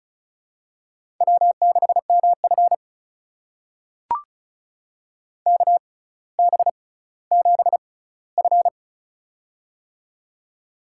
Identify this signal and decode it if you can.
Morse code, heard as a keyed tone, about 34 wpm, spaced as for 18 wpm, 700 Hz. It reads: W6MF KB7F
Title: USA Call Signs - 6x Character Spacing - 35wpm